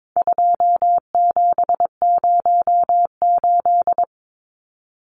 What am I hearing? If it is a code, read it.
Morse code, heard as a keyed tone, about 22 wpm, 700 Hz. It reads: 2708
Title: Numbers - 4-Digit 22wpm